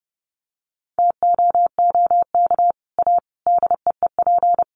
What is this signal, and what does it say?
Morse code, heard as a keyed tone, about 30 wpm, 700 Hz. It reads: TOOK A DEEP
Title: Sets of 5 Words Encore 30wpm